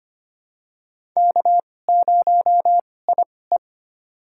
Morse code, heard as a keyed tone, about 25 wpm, spaced as for 18 wpm, 700 Hz. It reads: K0IE
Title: USA Call Signs - 2x Character Spacing - 25wpm